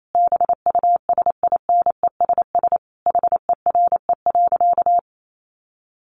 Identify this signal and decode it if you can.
Morse code, heard as a keyed tone, about 28 wpm, 700 Hz. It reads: BUSINESS HERE.
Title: Sentences from Top 1300 Words 28wpm